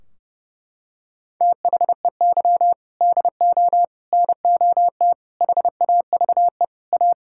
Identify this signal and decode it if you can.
Morse code, heard as a keyed tone, about 32 wpm, 700 Hz. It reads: THEY DO NOT HAVE A
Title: Sentences from Top 1600 Words 30wpm